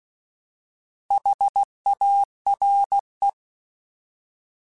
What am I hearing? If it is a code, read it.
Morse code, heard as a keyed tone, about 16 wpm, 800 Hz. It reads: HARE
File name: Alphabetical / Alphabetical mp3s / Hs @15